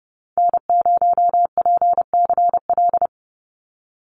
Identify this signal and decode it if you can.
Morse code, heard as a keyed tone, about 30 wpm, 700 Hz. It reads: N0PCL
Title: NA SOTA Activator Callsigns 30wpm